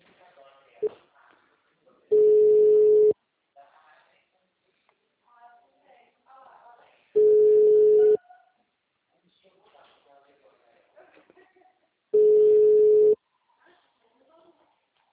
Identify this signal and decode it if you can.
Ringback tone (Europe, 425 Hz), with speech.
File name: out- -2011- - - .2